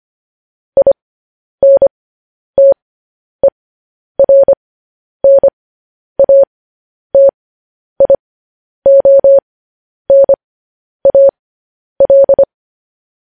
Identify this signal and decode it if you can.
Morse code, heard as a keyed tone, about 25 wpm, spaced as for 10 wpm, 550 Hz. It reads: INTERNATIONAL
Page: news headlines in morse code at 10 wpm